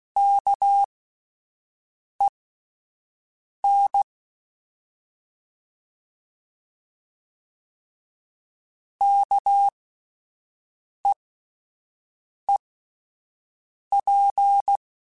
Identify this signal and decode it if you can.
Morse code, heard as a keyed tone, about 16 wpm, spaced as for 5 wpm, 800 Hz. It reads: KEN KEEP
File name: Alphabetical / Alphabetical mp3s / Ks @5